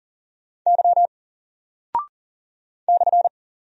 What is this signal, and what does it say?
Morse code, heard as a keyed tone, about 40 wpm, 700 Hz. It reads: Y /
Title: Single Character (Letter:Number:Punctuation:Prosign) 40wpm